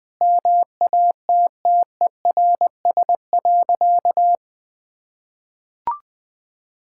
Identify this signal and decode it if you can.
Morse code, heard as a keyed tone, about 20 wpm, 700 Hz. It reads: MATTERS.
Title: Sentences with 6-Word Phrases - 20wpm